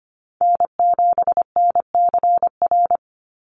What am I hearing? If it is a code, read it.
Morse code, heard as a keyed tone, about 25 wpm, 700 Hz. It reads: N7NCR
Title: USA Call Signs 25wpm